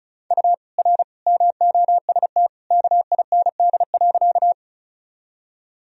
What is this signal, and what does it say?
Morse code, heard as a keyed tone, about 34 wpm, 700 Hz. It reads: U R MOST KIND.
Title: Sentences from Top 200 Words and Abbreviations 35wpm